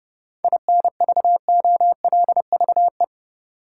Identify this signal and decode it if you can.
Morse code, heard as a keyed tone, about 30 wpm, 700 Hz. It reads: INVOLVE
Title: Sentences from Top 1200 Words - New Word Drill - 30wpm